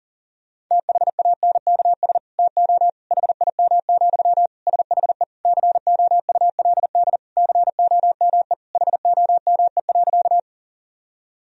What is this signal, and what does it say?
Morse code, heard as a keyed tone, about 37 wpm, 700 Hz. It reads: THANKS TO HIM, SHE COULD COME HOME.
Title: Sentences from Top 1000 Words - Review Part 3 - 40wpm